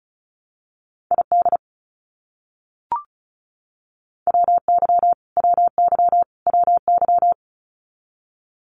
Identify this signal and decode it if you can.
Morse code, heard as a keyed tone, about 34 wpm, 700 Hz. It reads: ID WY WY WY
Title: US States Repeated 3x 35wpm